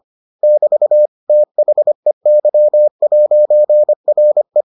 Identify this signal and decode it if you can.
Morse code, heard as a keyed tone, about 25 wpm, 600 Hz. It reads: <BT> THEY'RE